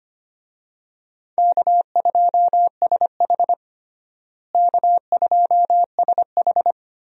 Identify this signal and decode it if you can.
Morse code, heard as a keyed tone, about 25 wpm, 700 Hz. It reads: K2SH K2SH
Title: USA Call Signs - 4 Characters - Repeated 3x - 25wpm